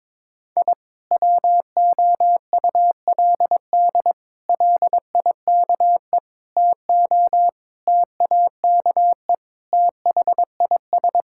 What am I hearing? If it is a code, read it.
Morse code, heard as a keyed tone, about 22 wpm, 700 Hz. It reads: I WOULD LIKE TO TAKE THIS
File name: practice / 22wpm / Sets of 8 Words - 22wpm